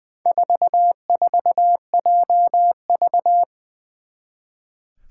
Morse code, heard as a keyed tone, about 20 wpm, 700 Hz. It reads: 44JV
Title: Four Challenging Letter-Number Combinations 20wpm